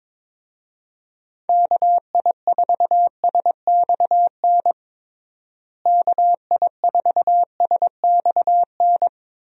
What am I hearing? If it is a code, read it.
Morse code, heard as a keyed tone, about 21 wpm, 700 Hz. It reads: KI4SXN KI4SXN
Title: USA Call Signs - 6 Characters - Repeated 3x - 22wpm